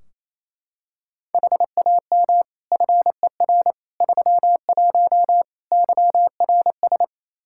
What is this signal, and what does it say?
Morse code, heard as a keyed tone, about 27 wpm, 700 Hz. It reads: HAM FER 31 YRS
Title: K1USN SST QSO Practice 28wpm